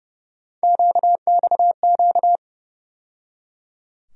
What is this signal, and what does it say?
Morse code, heard as a keyed tone, about 32 wpm, 700 Hz. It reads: QXQ
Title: USA Call Sign Postfixes - 3 Letters 30wpm